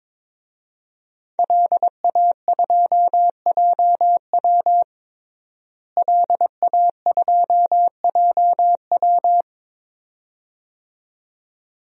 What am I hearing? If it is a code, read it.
Morse code, heard as a keyed tone, about 22 wpm, 700 Hz. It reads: LA2JW LA2JW
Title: DXCC-100 Call Signs - 5 Characters - Repeated 2x - 22wpm